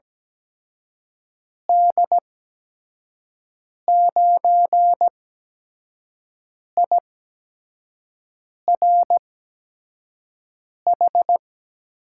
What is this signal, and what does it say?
Morse code, heard as a keyed tone, about 17 wpm, spaced as for 5 wpm, 700 Hz. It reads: D9IRH
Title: USA Call Signs - 6 Characters - 8x Character Spacing - 17wpm